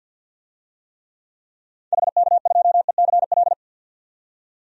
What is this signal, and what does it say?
Morse code, heard as a keyed tone, about 51 wpm, 700 Hz. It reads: HK2ECL
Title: DXCC-100 Call Signs - 6 Characters - 50wpm